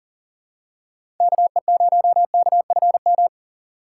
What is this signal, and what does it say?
Morse code, heard as a keyed tone, about 40 wpm, 700 Hz. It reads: KE0KFM
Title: USA Call Signs - 3x Character Spacing - 40wpm